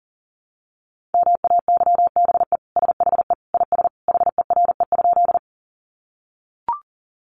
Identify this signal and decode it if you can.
Morse code, heard as a keyed tone, about 40 wpm, 700 Hz. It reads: MAYBE SHE IS HERE?
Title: Sentences from Top 1000 Words 40wpm